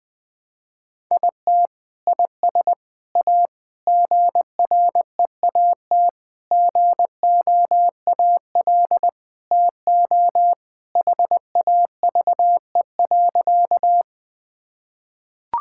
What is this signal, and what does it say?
Morse code, heard as a keyed tone, about 20 wpm, 700 Hz. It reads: IT IS A GREAT GOAL TO HAVE.
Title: Sentences from Top 1000 Words - Review Part 5 - 20wpm